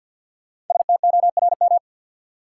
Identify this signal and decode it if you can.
Morse code, heard as a keyed tone, about 52 wpm, 700 Hz. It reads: STORM
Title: Sentences from Top 1500 Words - New Word Drill Alt - 50wpm